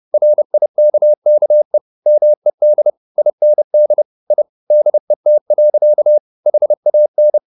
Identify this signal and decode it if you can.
Morse code, heard as a keyed tone, about 30 wpm, 600 Hz. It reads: RIKKE MED IND I DET. HAN
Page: Ferie 30 wpm